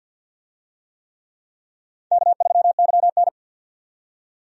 Morse code, heard as a keyed tone, about 48 wpm, 700 Hz. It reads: K3YN
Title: USA Call Signs - 4 Characters - 50wpm